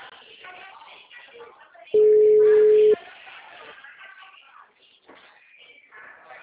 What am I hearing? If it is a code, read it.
Ringback tone (Europe, 425 Hz), with speech.